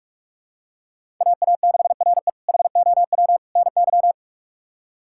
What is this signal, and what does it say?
Morse code, heard as a keyed tone, about 41 wpm, 700 Hz. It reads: AA6RE HOW NY
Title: K1USN SST QSO Practice - 45wpm